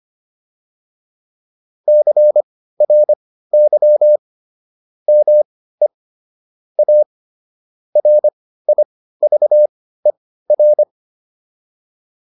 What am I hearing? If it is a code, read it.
Morse code, heard as a keyed tone, about 25 wpm, spaced as for 15 wpm, 600 Hz. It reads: CRY ME A RIVER